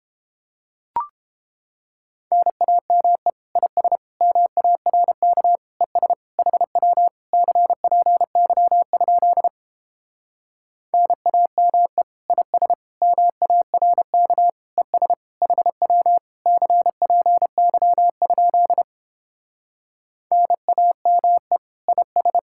Morse code, heard as a keyed tone, about 31 wpm, 700 Hz. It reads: NAME IS MARK ES HW CPY? NAME IS MARK ES HW CPY? NAME IS